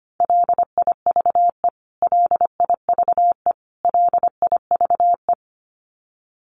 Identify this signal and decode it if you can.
Morse code, heard as a keyed tone, about 25 wpm, 700 Hz. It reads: LIVE LIVE LIVE
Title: Top 500 Words Repeated 25wpm